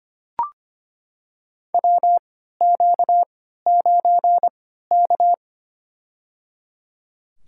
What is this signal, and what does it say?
Morse code, heard as a keyed tone, about 25 wpm, spaced as for 14 wpm, 700 Hz. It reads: WQ9K
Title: USA Call Signs - 4 Characters - 3x Character Spacing - 25wpm